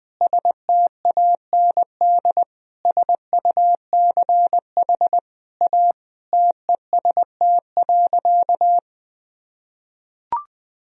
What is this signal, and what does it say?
Morse code, heard as a keyed tone, about 21 wpm, 700 Hz. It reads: STAND SUCH A TEST.
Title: Sentences from Top 1000 Words - Review Part 3 - 20wpm